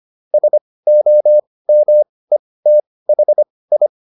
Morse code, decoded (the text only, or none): SOMETHI